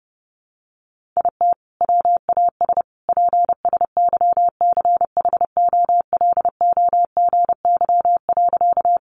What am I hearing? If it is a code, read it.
Morse code, heard as a keyed tone, about 30 wpm, 700 Hz. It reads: IT WAS PSYCHOLOGY.
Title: Sentences from Top 1400 Words 30wpm